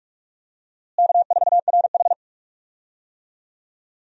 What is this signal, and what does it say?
Morse code, heard as a keyed tone, about 41 wpm, 700 Hz. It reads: K4RH
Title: USA Call Signs - 4 Characters - Repeated 2x - 45wpm